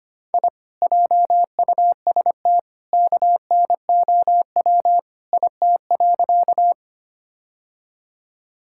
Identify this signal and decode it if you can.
Morse code, heard as a keyed tone, about 25 wpm, 700 Hz. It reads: I JUST KNOW IT.
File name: practice / 25wpm / Sentences with 2-Word Phrases - 25wpm